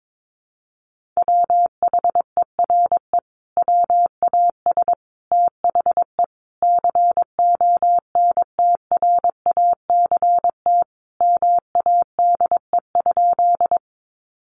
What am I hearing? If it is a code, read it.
Morse code, heard as a keyed tone, about 22 wpm, 700 Hz. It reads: WHERE WAS THE CONTRACT MADE?
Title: Sentences from Top 1400 Words 22wpm